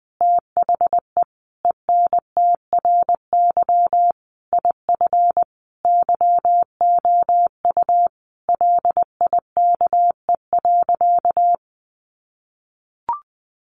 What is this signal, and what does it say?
Morse code, heard as a keyed tone, about 20 wpm, 700 Hz. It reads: THE ENTRY IF YOU LIKE.
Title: Sentences from Top 1500 Words 20wpm